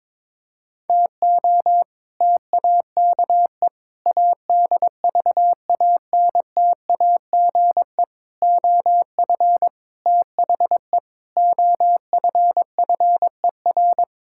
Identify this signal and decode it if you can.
Morse code, heard as a keyed tone, about 22 wpm, 700 Hz. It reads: TO TAKE ADVANTAGE OF THE OFFER
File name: practice / 22wpm / Sentences with 4-Word Phrases - 22wpm